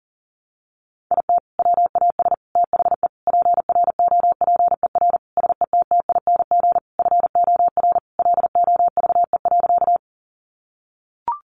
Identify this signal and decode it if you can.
Morse code, heard as a keyed tone, about 37 wpm, 700 Hz. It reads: IT WAS THE PROPER SETTING FOR LOVE.
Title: Sentences from Top 1300 Words 40wpm